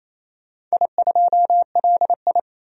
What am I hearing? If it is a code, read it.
Morse code, heard as a keyed tone, about 28 wpm, 700 Hz. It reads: I2LI
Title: DXCC-100 Call Signs - 4 Characters - 28wpm